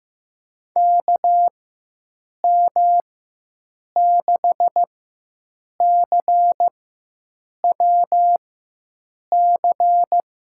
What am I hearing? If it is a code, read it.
Morse code, heard as a keyed tone, about 15 wpm, spaced as for 7 wpm, 700 Hz. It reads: KM6CWC
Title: USA Call Signs - 6 Characters - 4x Character Spacing - 15wpm